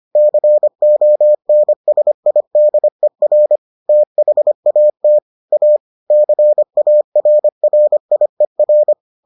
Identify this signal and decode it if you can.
Morse code, heard as a keyed tone, about 25 wpm, 600 Hz. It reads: CONSIDER THAT A CARRIER